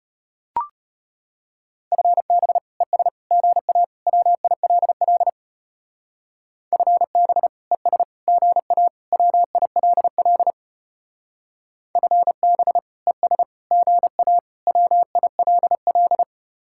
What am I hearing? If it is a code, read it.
Morse code, heard as a keyed tone, about 32 wpm, 700 Hz. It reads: FB ES GA WILL FB ES GA WILL FB ES GA WILL